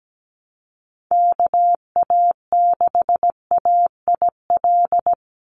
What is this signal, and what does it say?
Morse code, heard as a keyed tone, about 17 wpm, 700 Hz. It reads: KA6AIL
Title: Call Signs - 17wpm